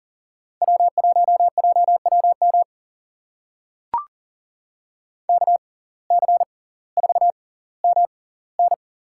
Morse code, heard as a keyed tone, about 40 wpm, spaced as for 20 wpm, 700 Hz. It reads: W1JWM KC4MN
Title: USA Call Signs - 5 Characters - 6x Character Spacing - 40wpm